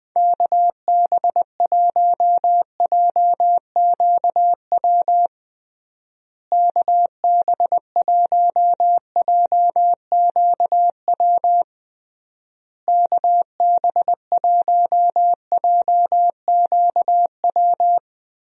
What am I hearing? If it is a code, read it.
Morse code, heard as a keyed tone, about 21 wpm, 700 Hz. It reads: KB1JQW KB1JQW KB1JQW
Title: USA Call Signs - 6 Characters - Repeated 3x - 20wpm